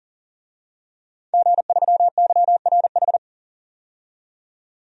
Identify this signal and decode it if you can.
Morse code, heard as a keyed tone, about 40 wpm, 700 Hz. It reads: G3YRH
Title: DXCC-100 Call Signs - 5 Characters - 40wpm